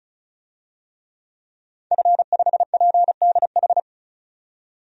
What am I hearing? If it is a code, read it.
Morse code, heard as a keyed tone, about 36 wpm, 700 Hz. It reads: F5PDH
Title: DXCC-100 Call Signs - 5 Characters - 35wpm